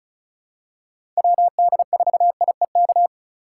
Morse code, heard as a keyed tone, about 36 wpm, 700 Hz. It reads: WD4IEK
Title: USA Call Signs - 6 Characters - 3x Character Spacing - 35wpm